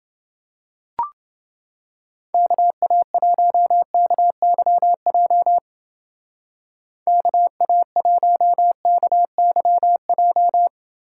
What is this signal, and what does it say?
Morse code, heard as a keyed tone, about 30 wpm, 700 Hz. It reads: KA1KYJ KA1KYJ